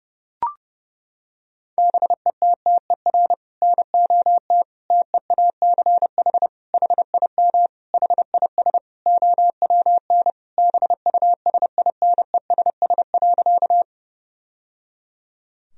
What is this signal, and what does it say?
Morse code, heard as a keyed tone, about 32 wpm, 700 Hz. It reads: BETTER NOT TEACH HIM HIS OWN BUSINESS.
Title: Sentences from Top 1000 Words - Review Part 2 - 30wpm